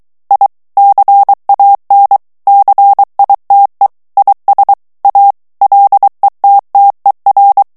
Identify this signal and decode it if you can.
Morse code, heard as a keyed tone, about 23 wpm, 800 Hz. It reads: I CAN CITE IS A LETTER
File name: morse mp3 / ANGLAIS texte clair 2 vitesse 22 mots minute